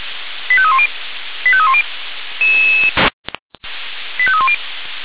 Five-tone selective call (ZVEI), heard as quick runs of five tones, tones of about 70 ms. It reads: 85310 85310 85310